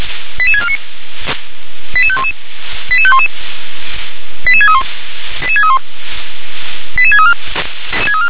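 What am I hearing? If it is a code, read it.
Five-tone selective call (ZVEI), heard as quick runs of five tones, tones of about 70 ms. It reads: 88649 88511 88511 88631 88631 88634
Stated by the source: ZVEI five-tone call 88730 received by SDR.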